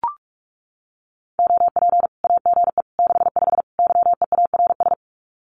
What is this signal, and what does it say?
Morse code, heard as a keyed tone, about 44 wpm, 700 Hz. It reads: OP AGE 65 YEARS
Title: Common QSO Phrases 45wpm